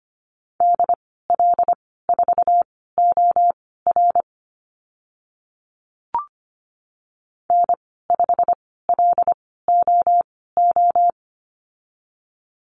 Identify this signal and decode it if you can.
Morse code, heard as a keyed tone, about 25 wpm, spaced as for 16 wpm, 700 Hz. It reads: DL4OR N5LOO
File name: Lesson 15g - Number 5 - Call Signs 25wpm